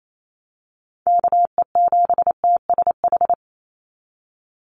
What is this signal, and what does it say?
Morse code, heard as a keyed tone, about 28 wpm, 700 Hz. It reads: KE7TSH
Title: USA Call Signs 28wpm